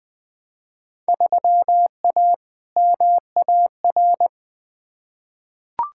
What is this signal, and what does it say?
Morse code, heard as a keyed tone, about 20 wpm, 700 Hz. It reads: 3A MAR
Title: Field Day Exchanges Encore - 20wpm